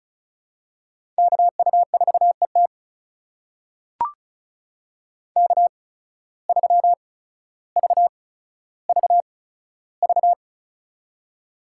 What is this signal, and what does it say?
Morse code, heard as a keyed tone, about 36 wpm, spaced as for 15 wpm, 700 Hz. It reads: KU4ET K3VVV
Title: USA Call Signs - 5 Characters - 8x Character Spacing - 35wpm